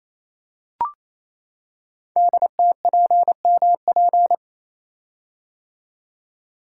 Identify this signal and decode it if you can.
Morse code, heard as a keyed tone, about 27 wpm, 700 Hz. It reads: DTPMP
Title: Code Groups - 5 Letters - 28wpm